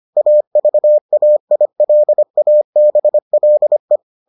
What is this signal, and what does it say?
Morse code, heard as a keyed tone, about 25 wpm, 600 Hz. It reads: AVAILABLE